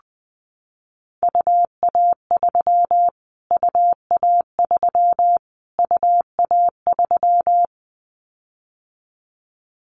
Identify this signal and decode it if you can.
Morse code, heard as a keyed tone, about 20 wpm, 700 Hz. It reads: UA3 UA3 UA3
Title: Callsign Prefixes for Top 25 DXCC - Repeated 3x - 20wpm